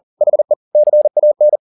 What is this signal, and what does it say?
Morse code, heard as a keyed tone, about 40 wpm, 600 Hz. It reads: HE CAN